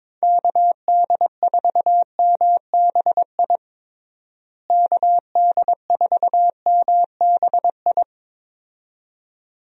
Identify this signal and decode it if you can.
Morse code, heard as a keyed tone, about 22 wpm, 700 Hz. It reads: KD4MBI KD4MBI
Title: USA Call Signs - 6 Characters - Repeated 2x - 22wpm